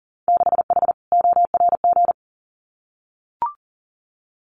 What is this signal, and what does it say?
Morse code, heard as a keyed tone, about 40 wpm, 700 Hz. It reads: 6H ORG
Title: Winter Field Day Exchanges Encore 40wpm